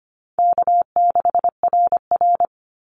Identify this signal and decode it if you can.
Morse code, heard as a keyed tone, about 25 wpm, 700 Hz. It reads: K6RR
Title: Easy Call Signs Repeated 2x - 25wpm